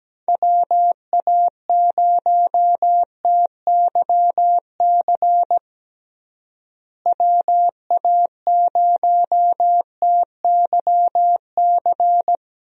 Morse code, heard as a keyed tone, about 17 wpm, 700 Hz. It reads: WA0TYC WA0TYC
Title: USA Call Signs - 6 Characters - Repeated 3x - 17wpm